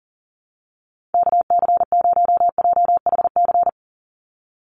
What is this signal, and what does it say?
Morse code, heard as a keyed tone, about 40 wpm, 690 Hz. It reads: KC0JHC
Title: USA Call Signs 40wpm